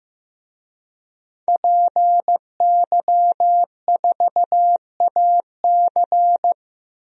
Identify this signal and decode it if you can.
Morse code, heard as a keyed tone, about 15 wpm, 700 Hz. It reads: PY4AC
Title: DXCC-100 Call Signs - 5 Characters - 15wpm